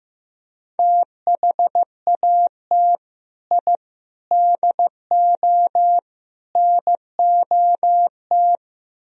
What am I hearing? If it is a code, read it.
Morse code, heard as a keyed tone, about 15 wpm, 700 Hz. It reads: THAT I DO NOT
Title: Speed-Racing - Sets of 4 Words 15wpm